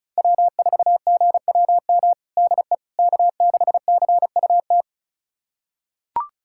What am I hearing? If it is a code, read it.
Morse code, heard as a keyed tone, about 34 wpm, 700 Hz. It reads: W4GWM DE K6CUT
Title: USA Callsign DE Callsign - 5 Characters - 35wpm